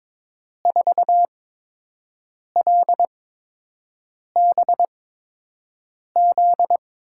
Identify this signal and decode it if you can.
Morse code, heard as a keyed tone, about 22 wpm, spaced as for 6 wpm, 700 Hz. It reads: 4LBZ
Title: USA Call Signs - 8x Character Spacing - 22wpm